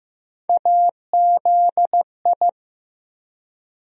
Morse code, heard as a keyed tone, about 15 wpm, 700 Hz. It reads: AZI
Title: USA Call Sign Postfixes - 3 Letters 15wpm